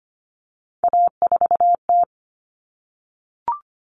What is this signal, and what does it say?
Morse code, heard as a keyed tone, about 25 wpm, 700 Hz. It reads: A4T
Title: All Cut Numbers - 3-Digit 25wpm